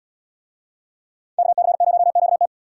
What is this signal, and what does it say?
Morse code, heard as a keyed tone, about 77 wpm, 700 Hz. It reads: KC0QT